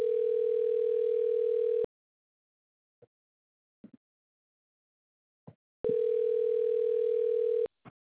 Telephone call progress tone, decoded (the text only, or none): ringback tone